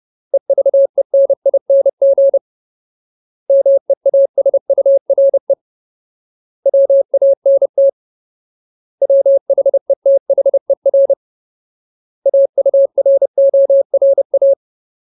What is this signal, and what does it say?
Morse code, heard as a keyed tone, about 30 wpm, 550 Hz. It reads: EVENING MEASURE WANT WHETHER AURORA
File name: Ultimate Word List / 30WPM Ultimate